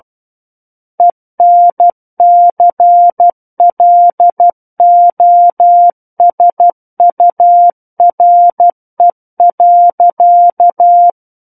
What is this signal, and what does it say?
Morse code, heard as a keyed tone, about 12 wpm, 700 Hz. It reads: ENCLOSURE.
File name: Book One / 03-Chapter 3 - 12 WPM